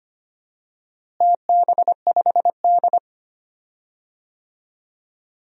Morse code, heard as a keyed tone, about 25 wpm, 700 Hz. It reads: TB5D
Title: DXCC-100 Call Signs - 4 Characters - Repeated 3x - 25wpm